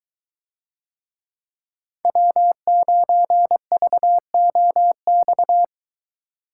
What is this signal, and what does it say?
Morse code, heard as a keyed tone, about 22 wpm, 700 Hz. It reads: W9VOX